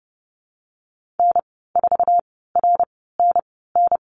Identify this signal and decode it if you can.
Morse code, heard as a keyed tone, about 32 wpm, spaced as for 17 wpm, 700 Hz. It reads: N4RNN
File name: Lesson 14g - Number 2 - Call Signs 30wpm